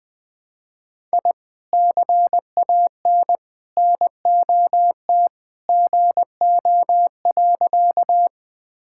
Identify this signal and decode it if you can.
Morse code, heard as a keyed tone, about 20 wpm, 700 Hz. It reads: I CAN NOT GO.
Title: Sentences with 2-Word Phrases - 20wpm